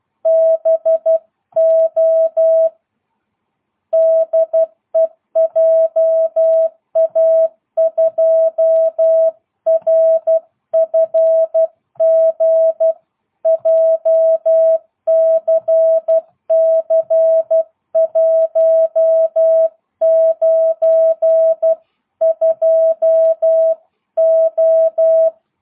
Morse code, heard as a keyed tone, about 12 wpm, 650 Hz. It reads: BO DE JA2RFG JCC192O
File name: jh8cbh / 2022cw